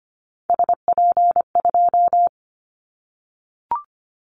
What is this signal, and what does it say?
Morse code, heard as a keyed tone, about 25 wpm, 700 Hz. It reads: SP2
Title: Callsign Prefixes for Top 25 DXCC - 25wpm